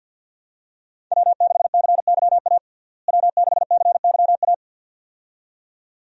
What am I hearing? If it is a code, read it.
Morse code, heard as a keyed tone, about 53 wpm, 700 Hz. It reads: W6CYA W6CYA
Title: USA Call Signs - 5 Characters - Repeated 2x - 50wpm